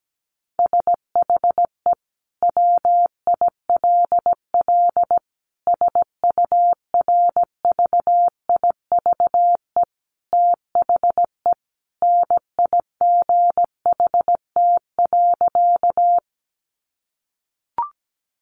Morse code, heard as a keyed tone, about 17 wpm, 700 Hz. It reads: SHE WILL SURVIVE THE NIGHT.
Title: Sentences from Top 1300 Words 17wpm